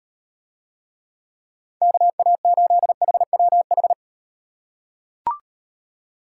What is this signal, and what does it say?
Morse code, heard as a keyed tone, about 38 wpm, 700 Hz. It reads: KA8HWH